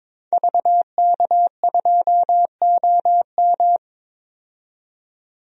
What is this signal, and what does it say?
Morse code, heard as a keyed tone, about 22 wpm, 700 Hz. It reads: VK2OM
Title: DXCC-100 Call Signs - 5 Characters - 22wpm